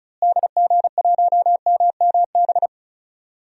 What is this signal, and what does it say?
Morse code, heard as a keyed tone, about 36 wpm, 700 Hz. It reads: DG1MMB
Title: DXCC-100 Call Signs - 6 Characters - Repeated 2x - 35wpm